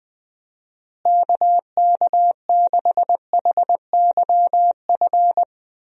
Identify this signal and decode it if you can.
Morse code, heard as a keyed tone, about 21 wpm, 700 Hz. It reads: KK6HYF
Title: USA Call Signs - 6 Characters - Repeated 3x - 20wpm